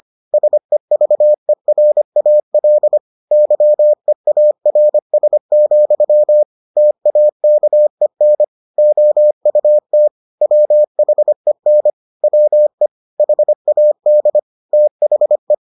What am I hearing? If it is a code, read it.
Morse code, heard as a keyed tone, about 25 wpm, 600 Hz. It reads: SEVERAL YEARS, TAKEN OUT WHEN WE HAD THE